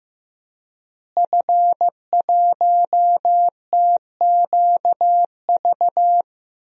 Morse code, heard as a keyed tone, about 15 wpm, 700 Hz. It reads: F1TQV
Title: DXCC-100 Call Signs - 5 Characters - Repeated 2x - 15wpm